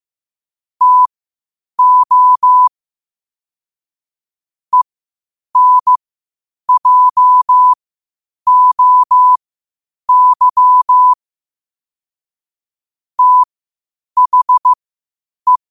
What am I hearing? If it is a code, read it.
Morse code, heard as a keyed tone, about 15 wpm, spaced as for 8 wpm, 1000 Hz. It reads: TO ENJOY THE
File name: qotdfiles / 8wpm